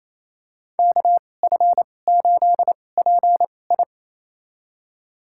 Morse code, heard as a keyed tone, about 28 wpm, spaced as for 20 wpm, 700 Hz. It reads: KF8PI
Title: USA Call Signs - 5 Characters - 2x Character Spacing - 28wpm